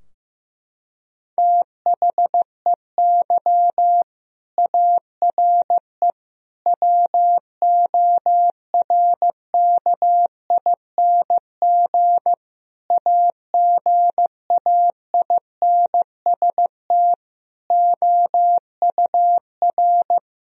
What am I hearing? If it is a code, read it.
Morse code, heard as a keyed tone, about 15 wpm, 700 Hz. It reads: THEY ARE WORKING AGAINST OUR
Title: Sentences from Top 1000 Words - Review Part 4 - 15wpm